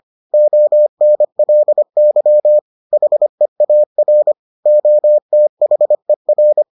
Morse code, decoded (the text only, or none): ONLY HEAR OTHER